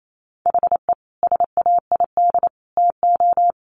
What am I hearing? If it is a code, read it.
Morse code, heard as a keyed tone, about 28 wpm, 700 Hz. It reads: HE SAID TO
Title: Sentences from Top 1100 Words 28wpm